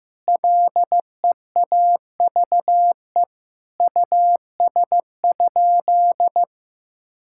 Morse code, heard as a keyed tone, about 15 wpm, 700 Hz. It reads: LEAVE US?
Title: Sentences from Top 1600 Words 15wpm